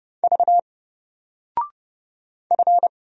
Morse code, heard as a keyed tone, about 30 wpm, 700 Hz. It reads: V F
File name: Single-Letter-Number-30wpm